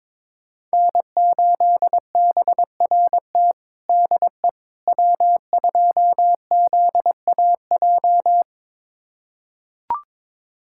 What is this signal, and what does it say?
Morse code, heard as a keyed tone, about 21 wpm, 700 Hz. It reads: N8BRT DE W2ZAJ
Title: USA Callsign DE Callsign - 5 Characters - 22wpm